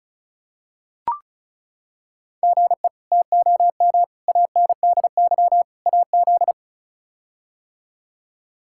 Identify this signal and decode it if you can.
Morse code, heard as a keyed tone, about 36 wpm, 700 Hz. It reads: GE TOM ANDY AZ
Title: K1USN SST QSO Practice - 35wpm